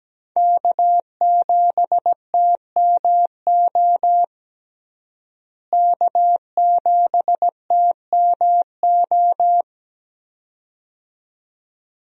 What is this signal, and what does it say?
Morse code, heard as a keyed tone, about 17 wpm, 700 Hz. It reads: K7TMO K7TMO
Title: USA Call Signs - 5 Characters - Repeated 2x - 17wpm